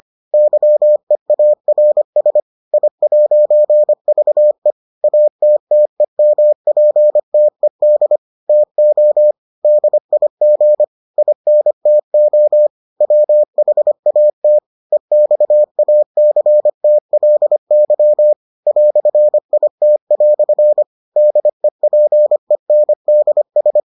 Morse code, heard as a keyed tone, about 25 wpm, 600 Hz. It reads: YEARS I'VE ATTEMPTED TO DIG INTO WHAT EXACTLY "IT" DEPENDS